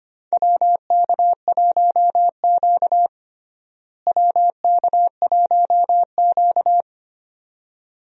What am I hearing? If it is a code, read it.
Morse code, heard as a keyed tone, about 25 wpm, 700 Hz. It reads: WK1Q WK1Q
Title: USA Call Signs - 4 Characters - Repeated 2x - 25wpm